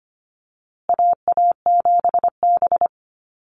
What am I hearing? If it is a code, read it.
Morse code, heard as a keyed tone, about 25 wpm, 700 Hz. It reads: AA7B
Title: USA Call Signs Repeated 2x 25wpm